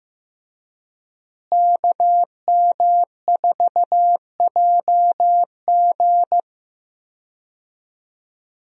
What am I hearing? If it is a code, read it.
Morse code, heard as a keyed tone, about 15 wpm, 700 Hz. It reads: KM4JG
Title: USA Call Signs - 5 Characters - Repeated 2x - 15wpm